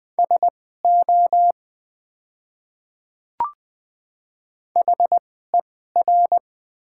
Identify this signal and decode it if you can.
Morse code, heard as a keyed tone, about 20 wpm, spaced as for 14 wpm, 700 Hz. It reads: SO HER
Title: 2x Character Spacing - Top 100 Words - 20wpm